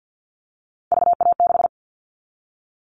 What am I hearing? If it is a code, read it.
Morse code, heard as a keyed tone, about 50 wpm, 700 Hz. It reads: VA6
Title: Callsign Prefixes for Top 100 DXCC - 50wpm